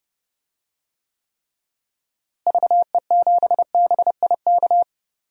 Morse code, heard as a keyed tone, about 32 wpm, 700 Hz. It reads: VE7BIK
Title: DXCC-100 Call Signs - 6 Characters - Repeated 3x - 30wpm